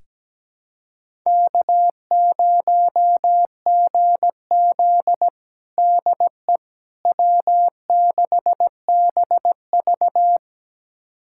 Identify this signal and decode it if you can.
Morse code, heard as a keyed tone, about 17 wpm, 700 Hz. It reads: K0GZ DE W6BV
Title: USA Callsign DE Callsign - 4 Characters - 17wpm